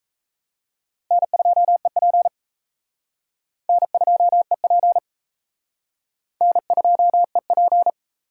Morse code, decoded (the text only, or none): N2EP N2EP N2EP